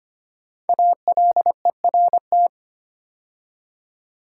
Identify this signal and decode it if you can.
Morse code, heard as a keyed tone, about 26 wpm, 700 Hz. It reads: ALERT
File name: 5 Letter Words - First 3 Letters Given - 25wpm